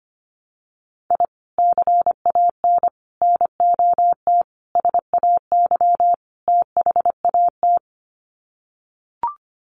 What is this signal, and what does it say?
Morse code, heard as a keyed tone, about 25 wpm, 700 Hz. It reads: I CAN NOT SAY THAT
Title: Sets of 5 Words Encore2 25wpm